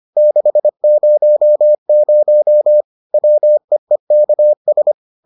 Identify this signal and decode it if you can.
Morse code, heard as a keyed tone, about 25 wpm, 600 Hz. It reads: 600 WEEKS